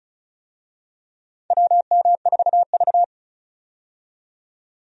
Morse code, heard as a keyed tone, about 36 wpm, 700 Hz. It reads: WM4V
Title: USA Call Signs - 4 Characters - 35wpm